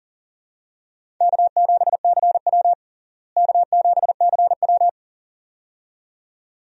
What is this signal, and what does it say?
Morse code, heard as a keyed tone, about 37 wpm, 700 Hz. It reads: K7CW K7CW
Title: USA Call Signs - 4 Characters - Repeated 2x - 40wpm